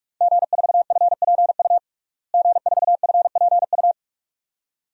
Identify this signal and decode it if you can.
Morse code, heard as a keyed tone, about 46 wpm, 700 Hz. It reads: G4FPU G4FPU
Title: DXCC-100 Call Signs - 5 Characters - Repeated 2x - 45wpm